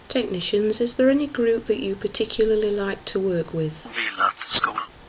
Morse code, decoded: none